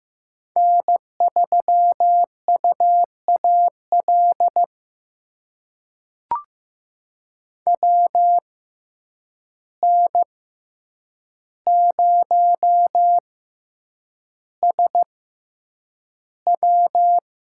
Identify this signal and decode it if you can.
Morse code, heard as a keyed tone, about 15 wpm, spaced as for 8 wpm, 700 Hz. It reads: N3UAL WN0SW
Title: USA Call Signs - 5 Characters - 6x Character Spacing - 15wpm